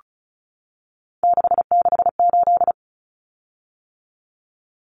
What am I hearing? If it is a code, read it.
Morse code, heard as a keyed tone, about 34 wpm, 700 Hz. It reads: 668
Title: Common Cut Numbers - 3-Digit 35wpm